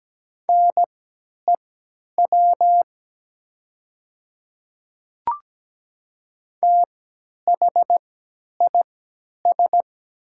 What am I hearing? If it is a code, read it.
Morse code, heard as a keyed tone, about 17 wpm, spaced as for 10 wpm, 700 Hz. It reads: NEW THIS
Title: 3x Character Spacing - Top 200 Words - 17wpm